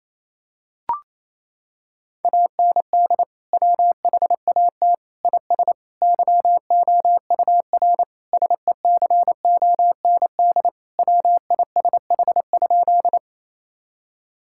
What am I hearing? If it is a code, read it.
Morse code, heard as a keyed tone, about 28 wpm, 700 Hz. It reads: AND WHAT IS YOUR SECOND WISH?
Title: Sentences from Top 1600 Words 28wpm